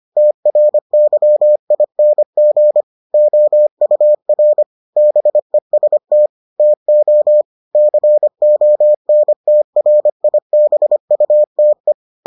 Morse code, heard as a keyed tone, about 25 wpm, 600 Hz. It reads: TRYING OUR BEST TO CONTRIBUTE